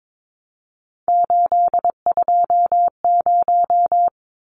Morse code, heard as a keyed tone, about 22 wpm, 700 Hz. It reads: 820
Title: Numbers - 3-Digit 22wpm